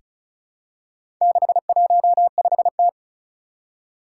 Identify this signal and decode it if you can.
Morse code, heard as a keyed tone, about 36 wpm, 700 Hz. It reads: 615T
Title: Four Challenging Letter-Number Combinations 35wpm